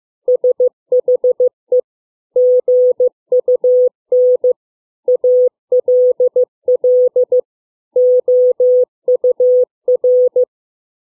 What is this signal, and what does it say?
Morse code, heard as a keyed tone, about 15 wpm, 500 Hz. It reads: SHE GUN ALL OUR